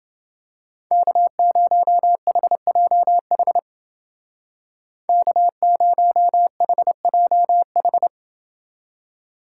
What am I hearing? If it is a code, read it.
Morse code, heard as a keyed tone, about 29 wpm, 700 Hz. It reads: K0HJH K0HJH